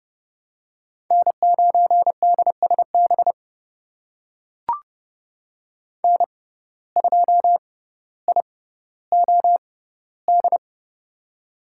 Morse code, heard as a keyed tone, about 31 wpm, spaced as for 16 wpm, 700 Hz. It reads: N9DSB N2IOD
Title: USA Call Signs - 5 Characters - 6x Character Spacing - 30wpm